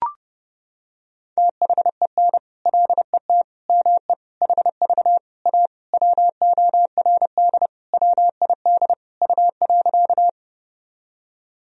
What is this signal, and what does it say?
Morse code, heard as a keyed tone, about 32 wpm, 700 Hz. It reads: THEN LET ME HV A WORD WID U.
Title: Sentences from Top 300 Words and Abbreviations 30wpm